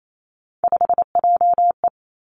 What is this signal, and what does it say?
Morse code, heard as a keyed tone, about 28 wpm, 700 Hz. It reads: 5JE
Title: Three Challenging Letter-Number Combinations 28wpm